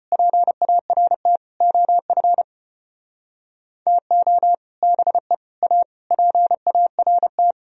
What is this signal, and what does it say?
Morse code, heard as a keyed tone, about 30 wpm, 700 Hz. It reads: PART OF TO BE A PART
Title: Speed-Racing - Sets of 5 Words 25wpm